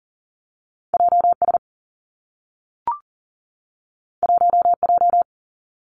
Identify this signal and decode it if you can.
Morse code, heard as a keyed tone, about 40 wpm, 700 Hz. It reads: JS 1J
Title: Two Challenging Letter-Number Combinations 40wpm